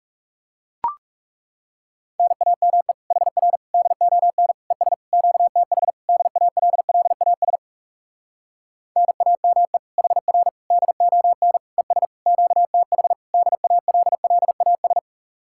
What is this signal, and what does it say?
Morse code, heard as a keyed tone, about 37 wpm, 700 Hz. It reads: NAME HR DON ES QTH DALLAS NAME HR DON ES QTH DALLAS
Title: Speed-Racing Common QSO Phrases Encore - 30wpm